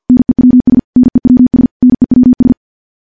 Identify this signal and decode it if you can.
Morse code, heard as a keyed tone, about 51 wpm, 250 Hz. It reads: NEWS NEWS NEWS
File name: morse / morse-training-50WPM